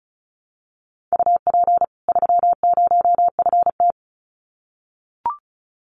Morse code, heard as a keyed tone, about 34 wpm, 700 Hz. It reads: UP 30FT
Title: Common QSO Phrases 35wpm